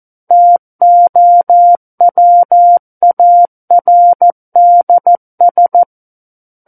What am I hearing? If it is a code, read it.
Morse code, heard as a keyed tone, about 14 wpm, 700 Hz. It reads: TOWARDS
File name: Book One / 05-Chapter 5 - 14 WPM